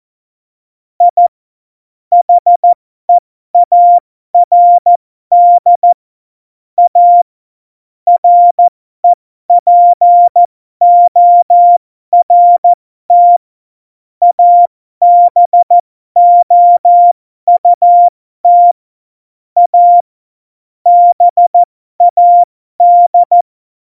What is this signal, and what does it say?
Morse code, heard as a keyed tone, about 14 wpm, spaced as for 12 wpm, 700 Hz. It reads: I HEARD A REPORT ABOUT A BAD
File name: Morsekurs / Witze